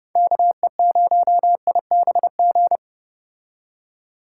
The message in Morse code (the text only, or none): KE0IBG